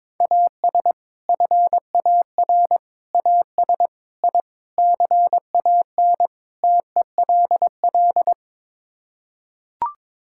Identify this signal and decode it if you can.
Morse code, heard as a keyed tone, about 22 wpm, 700 Hz. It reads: AS FAR AS I CAN TELL
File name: practice / 22wpm / Sets of 6 Words - 22wpm